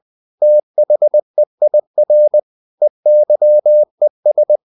Morse code, heard as a keyed tone, about 20 wpm, 600 Hz. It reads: THEIR EYES
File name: the old man and the sea / the old man and the sea part 1 20wpm